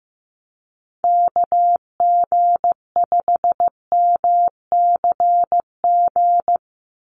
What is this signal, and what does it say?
Morse code, heard as a keyed tone, about 15 wpm, 700 Hz. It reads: KG5MCG
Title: USA Call Signs 15wpm